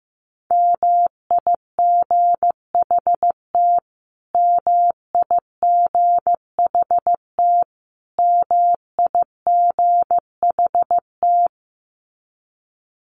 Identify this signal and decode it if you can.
Morse code, heard as a keyed tone, about 15 wpm, 700 Hz. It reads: MIGHT MIGHT MIGHT
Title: Top 500 Words Repeated 15wpm